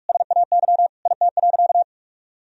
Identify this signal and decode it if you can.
Morse code, heard as a keyed tone, about 46 wpm, 700 Hz. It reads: SAY IT.
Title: Sentences from Top 400 Words and Abbreviations 45wpm